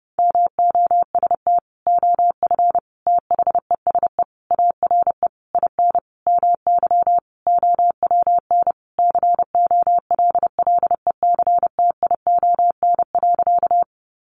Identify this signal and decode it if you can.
Morse code, heard as a keyed tone, about 30 wpm, 700 Hz. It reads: MOST OF THESE ARE IN MY OWN COLLECTION.
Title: Sentences from Top 1200 Words 30wpm